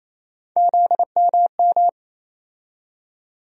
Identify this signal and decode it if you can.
Morse code, heard as a keyed tone, about 28 wpm, 700 Hz. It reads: ZMM
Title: USA Call Sign Postfixes - 3 Letters 28wpm